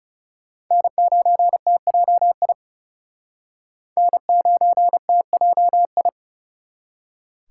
Morse code, heard as a keyed tone, about 40 wpm, 700 Hz. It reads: N9TJI N9TJI